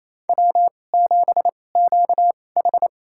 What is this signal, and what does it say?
Morse code, heard as a keyed tone, about 27 wpm, spaced as for 20 wpm, 700 Hz. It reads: W7QH
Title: USA Call Signs - 4 Characters - 2x Character Spacing - 28wpm